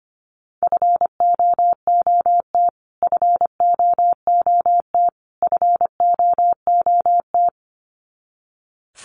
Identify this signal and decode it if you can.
Morse code, heard as a keyed tone, about 25 wpm, 700 Hz. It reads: FOOT FOOT FOOT
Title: Top 500 Words Repeated 25wpm